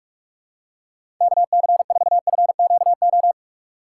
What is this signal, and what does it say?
Morse code, heard as a keyed tone, about 46 wpm, 700 Hz. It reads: KC4FQO
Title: USA Call Signs - 3x Character Spacing - 45wpm